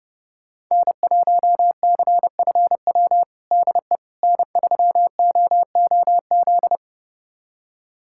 Morse code, heard as a keyed tone, about 30 wpm, 700 Hz. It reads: N1CFW DE N3OOZ
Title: USA Callsign DE Callsign 30wpm